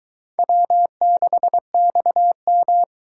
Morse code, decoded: W6XM

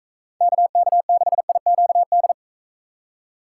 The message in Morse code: KK6IQD

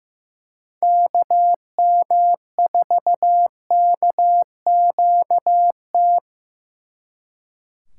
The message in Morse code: KM4KQT

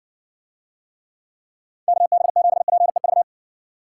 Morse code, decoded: KB7PEV

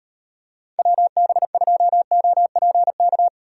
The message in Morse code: WB2OPK

